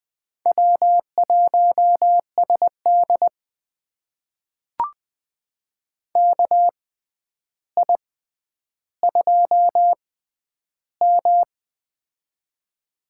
Morse code, heard as a keyed tone, about 20 wpm, spaced as for 10 wpm, 700 Hz. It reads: W1SD KI2M
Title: USA Call Signs - 6x Character Spacing - 20wpm